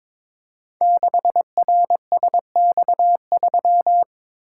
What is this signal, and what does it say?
Morse code, heard as a keyed tone, about 22 wpm, 700 Hz. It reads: 6RSX3
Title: Code Groups - 5 Letter:Numbers - 22wpm